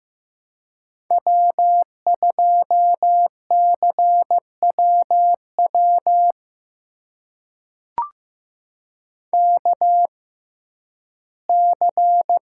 Morse code, heard as a keyed tone, about 15 wpm, spaced as for 11 wpm, 700 Hz. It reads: W2CWW KC
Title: USA Call Signs - 6x Character Spacing - 15wpm